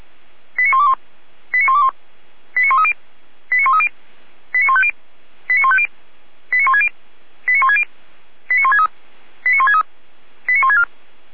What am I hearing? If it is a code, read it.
Five-tone selective call (ZVEI), heard as quick runs of five tones, tones of about 80 ms. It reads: 89121 89121 89130 89130 89150 89150 89160 89160 89163 89163 89164
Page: /455/SDR/ZVEI/